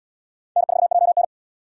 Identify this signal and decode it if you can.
Morse code, heard as a keyed tone, about 78 wpm, 700 Hz. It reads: N4JN